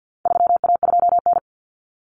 Morse code, heard as a keyed tone, about 50 wpm, 700 Hz. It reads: 3A2N
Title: Common Cut Numbers - 4-Digit 50wpm